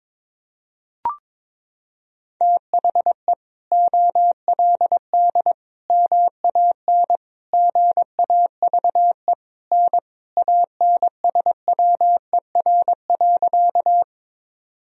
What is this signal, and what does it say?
Morse code, heard as a keyed tone, about 22 wpm, 700 Hz. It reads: THE OLD MAN GAVE N ANSWER.
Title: Sentences from Top 500 Words and Abbreviations 22wpm